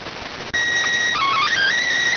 SSTV (slow-scan television), PD 290.